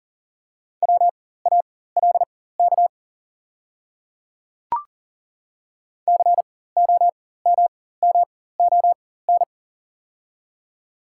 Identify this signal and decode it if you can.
Morse code, heard as a keyed tone, about 40 wpm, spaced as for 19 wpm, 700 Hz. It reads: WALK COMMON
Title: 4x Character Spacing - Top 500 Words - 40wpm